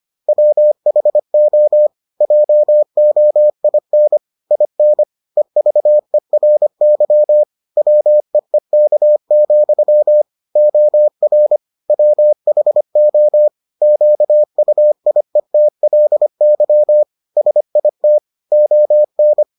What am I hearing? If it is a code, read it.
Morse code, heard as a keyed tone, about 25 wpm, 600 Hz. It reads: WHO JOIN IN EVERY WEEK, OR WHO QUIETLY SIT ON